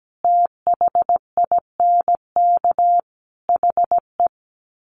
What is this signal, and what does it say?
Morse code, heard as a keyed tone, about 17 wpm, 700 Hz. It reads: THINK HE
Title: Sets of 5 Words 17wpm